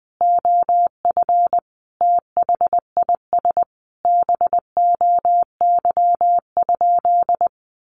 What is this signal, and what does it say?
Morse code, heard as a keyed tone, about 20 wpm, 700 Hz. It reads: OF THIS BOY?
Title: Sentences from Top 1500 Words 20wpm